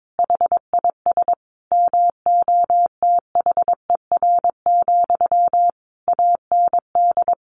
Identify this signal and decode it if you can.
Morse code, heard as a keyed tone, about 22 wpm, 700 Hz. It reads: HIS MOTHER, AND
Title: Sentences from Top 1200 Words 22wpm